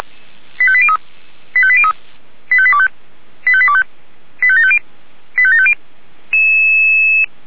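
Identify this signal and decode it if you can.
Five-tone selective call (ZVEI), heard as quick runs of five tones, tones of about 70 ms. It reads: 85803 85803 85825 85825 85750 85750
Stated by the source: ZVEI five-tone selective call 85726, received via SDR.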